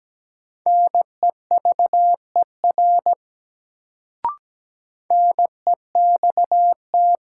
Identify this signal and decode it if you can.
Morse code, heard as a keyed tone, about 17 wpm, 700 Hz. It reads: NEVER NEXT
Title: Sentences from Top 300 Words - New Word Drill - 17wpm